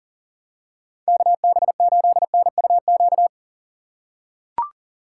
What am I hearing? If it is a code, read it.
Morse code, heard as a keyed tone, about 37 wpm, 700 Hz. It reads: KB8NUQ
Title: USA Call Signs - 6 Characters - 4x Character Spacing - 40wpm